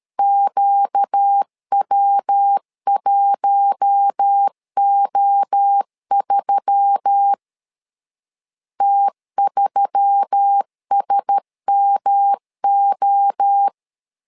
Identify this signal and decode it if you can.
Morse code, heard as a keyed tone, about 13 wpm, 800 Hz. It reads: QW1O3 T3SMO
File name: LEZIONI / 4 Q1S3 12wpm 2023